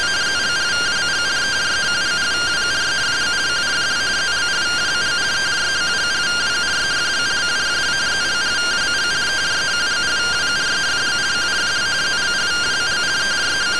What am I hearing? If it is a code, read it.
Two-tone teleprinter signal, about 170 Hz shift.